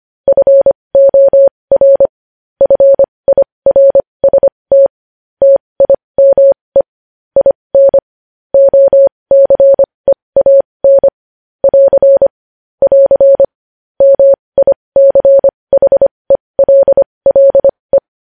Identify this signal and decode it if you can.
Morse code, heard as a keyed tone, about 25 wpm, spaced as for 20 wpm, 550 Hz. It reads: FOR FIRST TIME IN OCEAN <AR> <AR> MICHELLE
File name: podcast / News headlines-1x-25x20